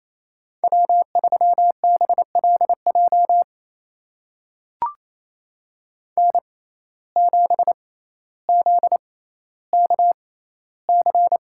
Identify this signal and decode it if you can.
Morse code, heard as a keyed tone, about 28 wpm, spaced as for 14 wpm, 700 Hz. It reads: W3BLJ N7ZKC
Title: USA Call Signs - 6x Character Spacing - 28wpm